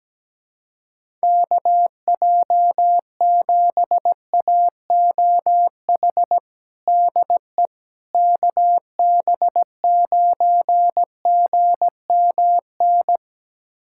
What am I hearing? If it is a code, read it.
Morse code, heard as a keyed tone, about 17 wpm, 700 Hz. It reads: KJ7AOH DE KB9GMN
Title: USA Callsign DE Callsign - 6 Characters - 17wpm